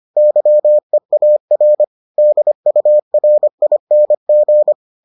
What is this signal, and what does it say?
Morse code, heard as a keyed tone, about 25 wpm, 600 Hz. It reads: YEAR DURING